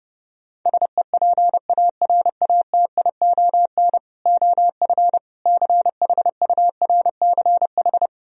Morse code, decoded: SEPARATION OF CHURCH

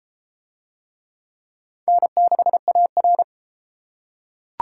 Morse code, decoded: N6AR

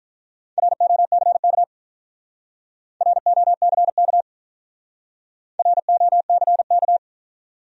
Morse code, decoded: ROCK ROCK ROCK

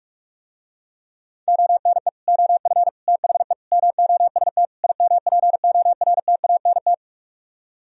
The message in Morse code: ONE OF THE MOST IMPORTANT

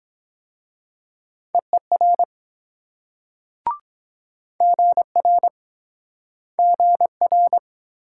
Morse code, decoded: EER GR GR

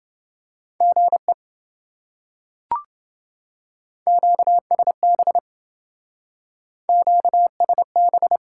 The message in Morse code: GE QSB QSB

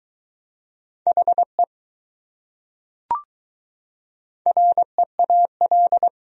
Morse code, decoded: HE REAL

